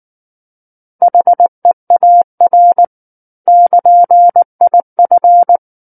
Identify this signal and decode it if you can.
Morse code, heard as a keyed tone, about 19 wpm, 700 Hz. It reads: HEAR <KN>IF
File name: Book One / 10-Chapter 10 - 19 WPM